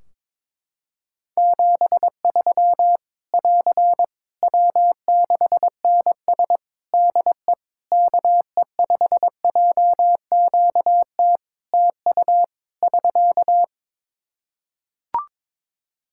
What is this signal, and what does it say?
Morse code, heard as a keyed tone, about 22 wpm, 700 Hz. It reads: 73 <AR> W6NS DE KE5JQT TU <SK>
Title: Practice QSO - Exchange 4 - Ending - 22wpm